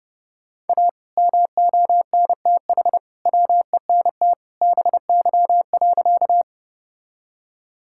Morse code, heard as a keyed tone, about 30 wpm, 700 Hz. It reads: A MONTH WENT BY.
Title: Sentences from Top 1000 Words - Review Part 5 - 30wpm